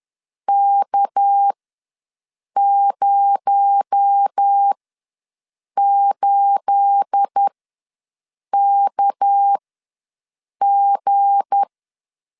Morse code, heard as a keyed tone, about 10 wpm, spaced as for 6 wpm, 800 Hz. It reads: K08KG
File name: LEZIONI / 9 28 ws 10wpm 4124